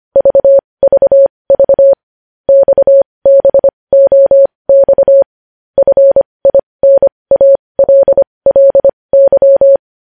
Morse code, heard as a keyed tone, about 25 wpm, spaced as for 20 wpm, 550 Hz. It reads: VVV XBOX FINALLY
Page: news-headlines-in-morse-code-at-20-wpm